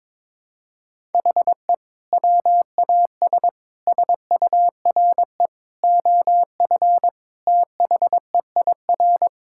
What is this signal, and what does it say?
Morse code, heard as a keyed tone, about 22 wpm, 700 Hz. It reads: HE WAS SURE OF THEIR
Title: Sentences from Top 1000 Words - Review Part 6 - 22wpm